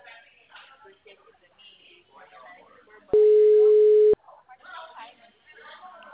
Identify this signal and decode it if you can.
Ringback tone (Europe, 425 Hz), with speech.